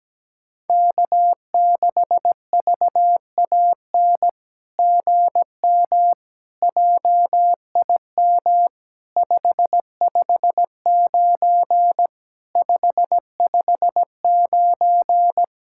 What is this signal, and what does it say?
Morse code, heard as a keyed tone, about 17 wpm, 700 Hz. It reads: K6VAN GM JIM 559 559
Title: SOTA QSO Practice 17wpm